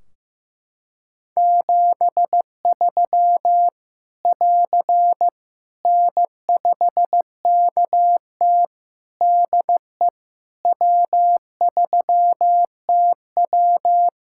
Morse code, heard as a keyed tone, about 15 wpm, 700 Hz. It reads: 73 <AR> N5KT DE W3TW
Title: Practice QSO - Exchange 4 - Ending - 15wpm